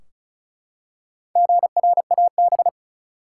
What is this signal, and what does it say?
Morse code, heard as a keyed tone, about 36 wpm, 700 Hz. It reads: GRAB
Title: Four Letter Words - 35wpm